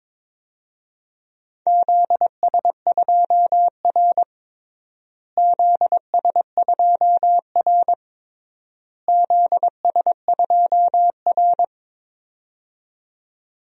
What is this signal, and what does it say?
Morse code, heard as a keyed tone, about 22 wpm, 700 Hz. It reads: ZS2R ZS2R ZS2R
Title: DXCC-100 Call Signs - 4 Characters - Repeated 3x - 22wpm